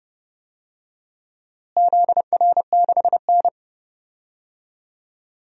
Morse code, heard as a keyed tone, about 30 wpm, 700 Hz. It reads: ZR6N
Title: DXCC-100 Call Signs - 4 Characters - Repeated 3x - 30wpm